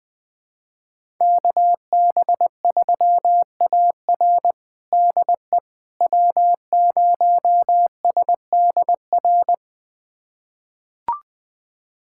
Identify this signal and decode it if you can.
Morse code, heard as a keyed tone, about 20 wpm, 700 Hz. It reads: KB3AR DE W0SDR
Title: USA Callsign DE Callsign - 5 Characters - 20wpm